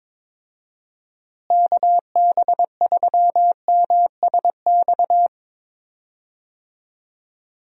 Morse code, heard as a keyed tone, about 21 wpm, 700 Hz. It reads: KB3MSX
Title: USA Call Signs - 6 Characters - Repeated 3x - 22wpm